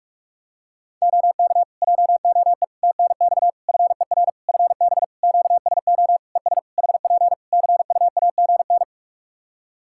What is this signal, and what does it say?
Morse code, heard as a keyed tone, about 44 wpm, 700 Hz. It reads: OK JOE TNX FER FB QSO ES HP CUAGN
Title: Practice QSO - Exchange 4 - Ending - 45wpm